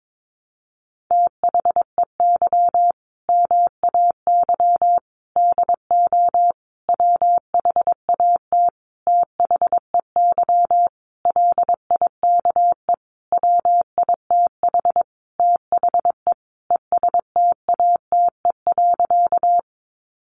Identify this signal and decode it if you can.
Morse code, heard as a keyed tone, about 22 wpm, 700 Hz. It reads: THEY MAY DO WHAT THEY LIKE WITH THE ESTATE.
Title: Sentences from Top 1200 Words 22wpm